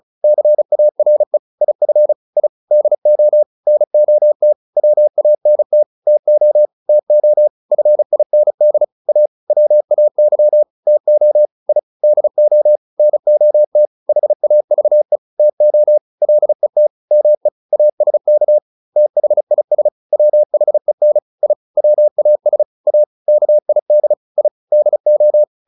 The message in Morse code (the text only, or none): CARE IF I DO NOT WANT TO TO FIND A WAY TO I DO NOT HAVE TO LET ME ASK THIS WHEN I WAS A KID I DO